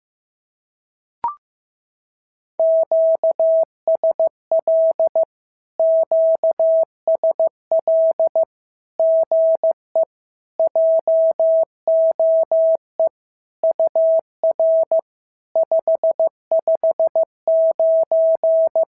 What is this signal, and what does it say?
Morse code, heard as a keyed tone, about 15 wpm, 650 Hz. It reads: QSL QSL GE JOE UR 559
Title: SOTA QSO Practice 15wpm